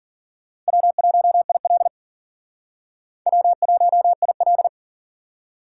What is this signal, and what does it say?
Morse code, heard as a keyed tone, about 50 wpm, 700 Hz. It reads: W1IL W1IL
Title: Speed Racing - USA 1x2 Call Signs 40wpm